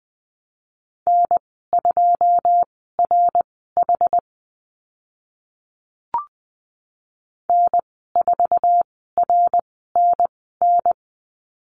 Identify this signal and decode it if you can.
Morse code, heard as a keyed tone, about 20 wpm, spaced as for 15 wpm, 700 Hz. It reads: N2RH N4RNN
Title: 082 Lesson 14g - Number 2 - Call Signs 20wpm